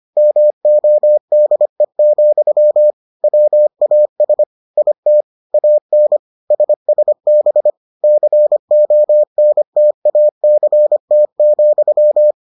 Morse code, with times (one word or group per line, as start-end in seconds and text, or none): 0.2-2.9 MODE,
3.2-4.4 WAS
4.8-5.2 IT
5.5-6.2 AN
6.5-7.7 SSB
8.0-12.3 CONTACT,